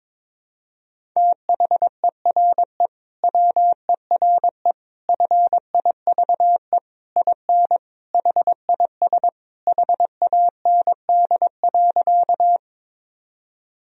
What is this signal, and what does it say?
Morse code, heard as a keyed tone, about 21 wpm, 700 Hz. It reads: THERE WERE FIVE IN HIS HAND.
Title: Sentences from Top 1000 Words - Review Part 2 - 22wpm